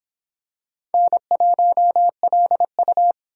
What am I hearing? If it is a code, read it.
Morse code, heard as a keyed tone, about 26 wpm, 700 Hz. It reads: N1LU